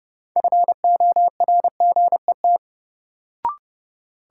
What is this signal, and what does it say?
Morse code, heard as a keyed tone, about 30 wpm, 700 Hz. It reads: FORGET
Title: Sentences from Top 1200 Words - New Word Drill Alt - 30wpm